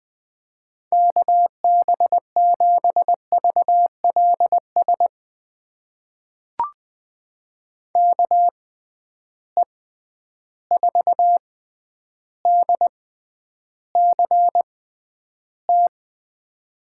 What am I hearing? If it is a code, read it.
Morse code, heard as a keyed tone, about 20 wpm, spaced as for 10 wpm, 700 Hz. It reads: KB7VLS KE4DCT
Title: USA Call Signs - 6 Characters - 6x Character Spacing - 20wpm